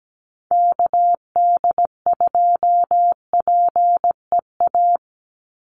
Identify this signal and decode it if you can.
Morse code, heard as a keyed tone, about 17 wpm, 700 Hz. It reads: KD2PEA
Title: USA Call Signs 17wpm